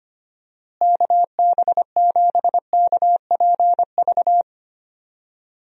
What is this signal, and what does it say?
Morse code, heard as a keyed tone, about 25 wpm, 700 Hz. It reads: KB7KPV
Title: USA Call Signs - 6 Characters - Repeated 3x - 25wpm